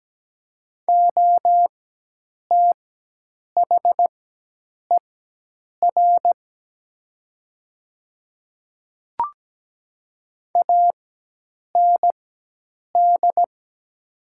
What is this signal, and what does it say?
Morse code, heard as a keyed tone, about 17 wpm, spaced as for 8 wpm, 700 Hz. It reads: OTHER AND
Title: 4x Character Spacing - Top 100 Words - 17wpm